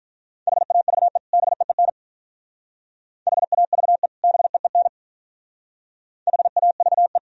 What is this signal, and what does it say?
Morse code, heard as a keyed tone, about 44 wpm, 700 Hz. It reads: HAVE BEEN HAVE BEEN HAVE
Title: Speed-Racing - Sets of 2 Words 35wpm